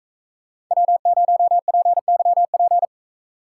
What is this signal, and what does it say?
Morse code, heard as a keyed tone, about 40 wpm, 700 Hz. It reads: W0PYP